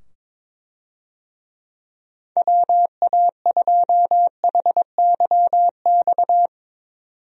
Morse code, heard as a keyed tone, about 22 wpm, 700 Hz. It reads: WA2HYX
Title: USA Call Signs - 6 Characters - 22wpm